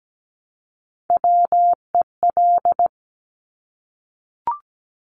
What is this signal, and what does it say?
Morse code, heard as a keyed tone, about 17 wpm, 700 Hz. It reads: WEL
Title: Top Three Letter Combinations in Words 17wpm